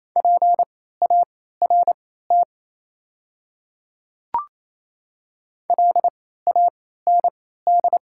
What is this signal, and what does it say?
Morse code, heard as a keyed tone, about 28 wpm, spaced as for 16 wpm, 690 Hz. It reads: PART LAND
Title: 3x Character Spacing - Top 200 Words - 28wpm